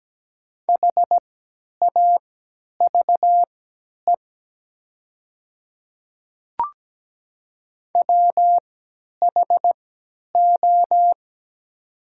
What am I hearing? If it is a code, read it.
Morse code, heard as a keyed tone, about 17 wpm, spaced as for 10 wpm, 690 Hz. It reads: HAVE WHO
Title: 3x Character Spacing - Top 100 Words - 17wpm